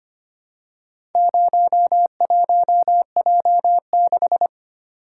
Morse code, heard as a keyed tone, about 25 wpm, 700 Hz. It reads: 01J6
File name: practice / 25wpm / Four Letter-Number Combinations - 25wpm